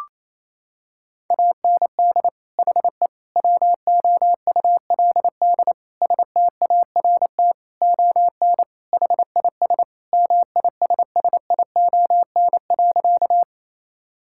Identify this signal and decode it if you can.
Morse code, heard as a keyed tone, about 28 wpm, 700 Hz. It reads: AND HE WOULD START ON HIS MISSION.
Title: Sentences from Top 1600 Words 28wpm